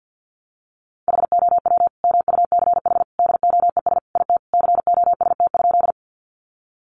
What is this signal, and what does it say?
Morse code, heard as a keyed tone, about 50 wpm, 700 Hz. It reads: HOW MUCH DOES IT COST?
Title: Sentences from Top 800 Words 50wpm